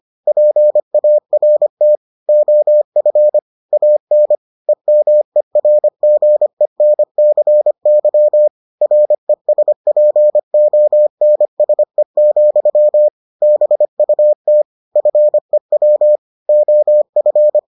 Morse code, heard as a keyed tone, about 25 wpm, 600 Hz. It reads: PART OF AN EMERGENCY RESPONSE, BUT FEW OF